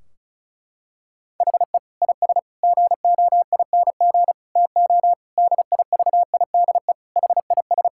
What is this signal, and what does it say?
Morse code, heard as a keyed tone, about 34 wpm, 700 Hz. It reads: HE IS GOING TO DIVIDE HIS
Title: Sentences from Top 1600 Words 35wpm